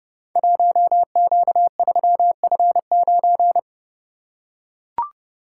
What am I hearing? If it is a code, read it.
Morse code, heard as a keyed tone, about 30 wpm, 700 Hz. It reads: 1Q3F9
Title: Code Groups - 5 Letter:Numbers - 30wpm